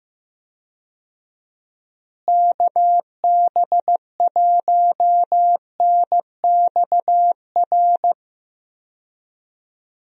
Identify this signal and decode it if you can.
Morse code, heard as a keyed tone, about 15 wpm, 700 Hz. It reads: KB1NXR
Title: USA Call Signs - 6 Characters - Repeated 3x - 15wpm